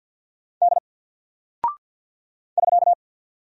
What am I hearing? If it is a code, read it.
Morse code, heard as a keyed tone, about 48 wpm, 700 Hz. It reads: D <SK>
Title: Single Character (Letter:Number:Punctuation:Prosign) Alt 50wpm